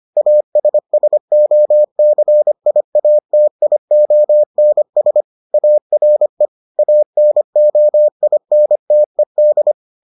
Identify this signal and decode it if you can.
Morse code, heard as a keyed tone, about 25 wpm, 600 Hz. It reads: ASSOCIATIONS ARE ANOINTED